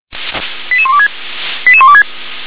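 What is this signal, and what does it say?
Five-tone selective call (ZVEI), heard as quick runs of five tones, tones of about 70 ms. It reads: 88126 88126
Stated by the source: Five-tone ZVEI selective call received on SDR.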